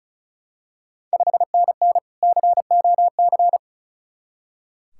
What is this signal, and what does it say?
Morse code, heard as a keyed tone, about 34 wpm, 700 Hz. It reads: 5NN COC
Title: Ontario QSO Party Exchange 35wpm